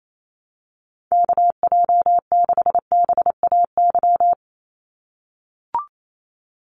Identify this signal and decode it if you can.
Morse code, heard as a keyed tone, about 28 wpm, 700 Hz. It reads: KJ6BAY
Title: USA Call Signs 28wpm